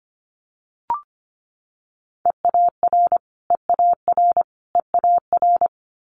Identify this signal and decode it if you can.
Morse code, heard as a keyed tone, about 25 wpm, 700 Hz. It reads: EAR EAR EAR
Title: Multi-letter Phonemes Repeated 3x 25wpm